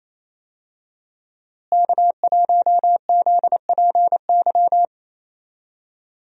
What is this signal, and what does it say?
Morse code, heard as a keyed tone, about 27 wpm, 700 Hz. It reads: K1ZPY
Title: USA Call Signs - 5 Characters - Repeated 3x - 28wpm